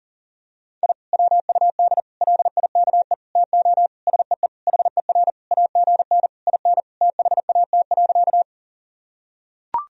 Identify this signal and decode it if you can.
Morse code, heard as a keyed tone, about 40 wpm, 700 Hz. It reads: I WUD LIKE TO SEE HER AGN IN THAT.
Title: Sentences from Top 200 Words and Abbreviations 40wpm